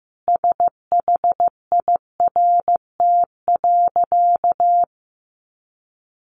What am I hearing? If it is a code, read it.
Morse code, heard as a keyed tone, about 15 wpm, 700 Hz. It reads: SHIRT.
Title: Additional Easy Phrases 15wpm